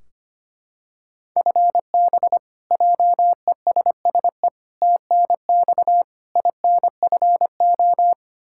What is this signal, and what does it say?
Morse code, heard as a keyed tone, about 25 wpm, 700 Hz. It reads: FB JESSE TNX INFO
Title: Practice QSO - Exchange 3 - More Info - 25wpm